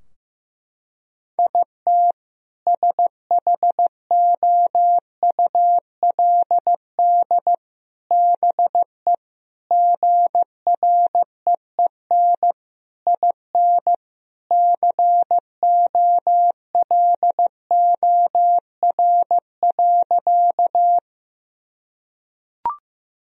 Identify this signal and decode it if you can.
Morse code, heard as a keyed tone, about 16 wpm, 700 Hz. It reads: IT SHOULD BE GREEN IN COLOR.
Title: Sentences from Top 1000 Words - Review Part 2 - 15wpm